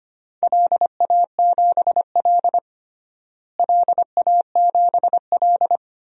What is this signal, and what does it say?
Morse code, heard as a keyed tone, about 25 wpm, 700 Hz. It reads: LA7L LA7L
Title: DXCC-100 Call Signs - 4 Characters - Repeated 2x - 25wpm